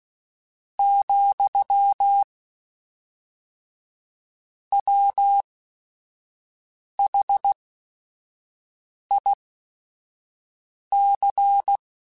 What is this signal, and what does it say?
Morse code, heard as a keyed tone, about 16 wpm, spaced as for 5 wpm, 800 Hz. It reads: , WHIC
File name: Technical Subjects / Technical Subjects-mp3s / XC @5